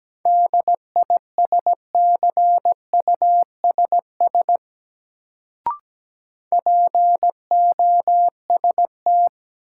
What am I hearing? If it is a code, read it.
Morse code, heard as a keyed tone, about 17 wpm, 700 Hz. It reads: DISCUSS POST
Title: Sentences from Top 1000 Words - New Word Drill - 17wpm